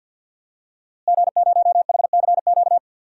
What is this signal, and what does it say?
Morse code, heard as a keyed tone, about 52 wpm, 700 Hz. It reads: G0HCQ
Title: DXCC-100 Call Signs - 5 Characters - Repeated 3x - 50wpm